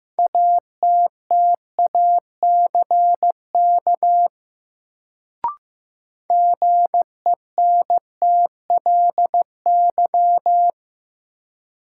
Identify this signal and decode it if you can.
Morse code, heard as a keyed tone, about 15 wpm, 700 Hz. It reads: ATTACK GENTLY
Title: Sentences from Top 1500 Words - New Word Drill Alt - 15wpm